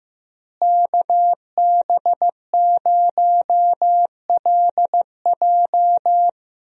Morse code, heard as a keyed tone, about 15 wpm, 700 Hz. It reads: KB0LJ
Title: USA Call Signs - 5 Characters - Repeated 2x - 15wpm